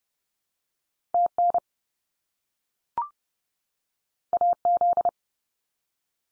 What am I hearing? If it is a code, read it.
Morse code, heard as a keyed tone, about 30 wpm, 700 Hz. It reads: TN AZ
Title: US States 30WPM